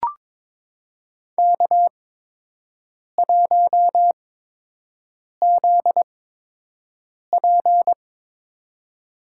Morse code, heard as a keyed tone, about 22 wpm, spaced as for 6 wpm, 700 Hz. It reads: K1ZP
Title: USA Call Signs - 4 Characters - 8x Character Spacing - 22wpm